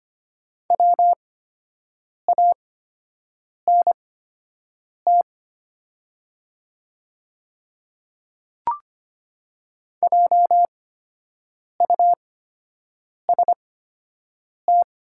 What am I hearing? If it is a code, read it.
Morse code, heard as a keyed tone, about 25 wpm, spaced as for 7 wpm, 700 Hz. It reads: WANT JUST
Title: 8x Character Spacing - Top 300 Words - 25wpm